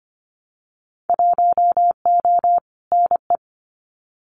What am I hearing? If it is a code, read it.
Morse code, heard as a keyed tone, about 25 wpm, 700 Hz. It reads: 1O NE
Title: Winter Field Day Exchanges Encore 25wpm